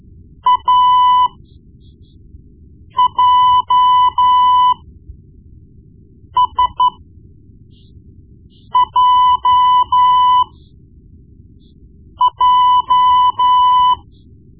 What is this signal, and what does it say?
Morse code, heard as a keyed tone, about 13 wpm, spaced as for 4 wpm, 1000 Hz. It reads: AJSJJ
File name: Morse Lessons / Morse Lessons / Tape-1B